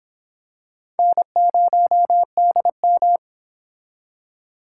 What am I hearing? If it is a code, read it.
Morse code, heard as a keyed tone, about 25 wpm, 700 Hz. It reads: N0DM